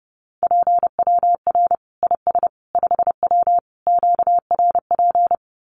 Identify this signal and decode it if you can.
Morse code, heard as a keyed tone, about 30 wpm, 700 Hz. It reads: PWR IS 5W QRP
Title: Common QSO Phrases 30wpm